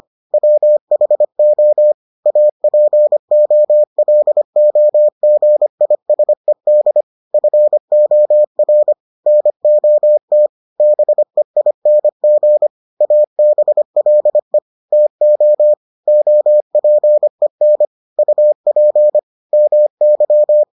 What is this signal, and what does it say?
Morse code, heard as a keyed tone, about 25 wpm, 600 Hz. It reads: WHO APOLOGISED FOR NOT BEING ABLE TO OPEN UP MY